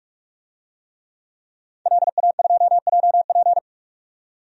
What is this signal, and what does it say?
Morse code, heard as a keyed tone, about 46 wpm, 700 Hz. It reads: LA2JP
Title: DXCC-100 Call Signs - 5 Characters - 45wpm